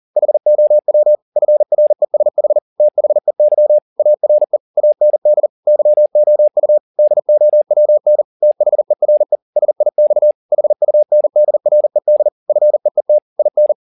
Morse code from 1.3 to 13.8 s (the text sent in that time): FRESH THEY ARE AND YOU DOWN THERE SIX HUNDRED FEET IN